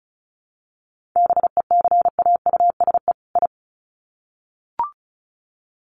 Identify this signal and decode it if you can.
Morse code, heard as a keyed tone, about 34 wpm, 700 Hz. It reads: BECAUSE I
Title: Sets of 2 Words 35wpm